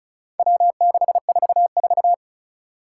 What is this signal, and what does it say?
Morse code, heard as a keyed tone, about 36 wpm, 700 Hz. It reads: W644
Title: Four Challenging Letter-Number Combinations 35wpm